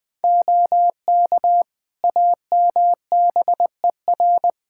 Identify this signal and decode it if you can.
Morse code, heard as a keyed tone, about 20 wpm, 700 Hz. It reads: OK AMBER
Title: Practice QSO - Exchange 4 - Ending - 20wpm